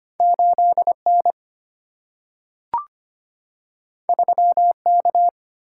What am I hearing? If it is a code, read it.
Morse code, heard as a keyed tone, about 25 wpm, 700 Hz. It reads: 8 N 3 K
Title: Single-Letter Single-Number 25wpm